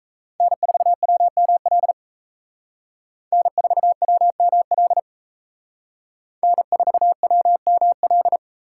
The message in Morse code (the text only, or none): N4WML N4WML N4WML